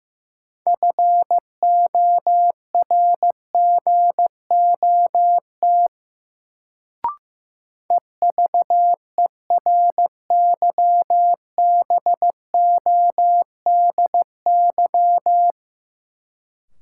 Morse code, heard as a keyed tone, about 15 wpm, 700 Hz. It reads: FORGOT EVERYBODY
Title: Sentences from Top 1400 Words - New Word Drill Alt - 15wpm